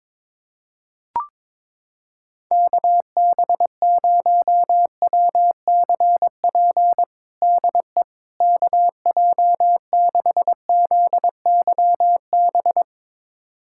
Morse code, heard as a keyed tone, about 22 wpm, 700 Hz. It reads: KB0WCP DE KJ6ZYB
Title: USA Callsign DE Callsign - 6 Characters - 22wpm